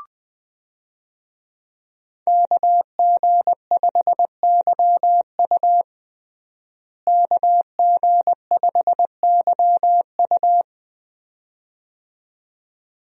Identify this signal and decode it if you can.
Morse code, heard as a keyed tone, about 20 wpm, 700 Hz. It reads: KG5YU KG5YU
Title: USA Call Signs - 5 Characters - Repeated 2x - 20wpm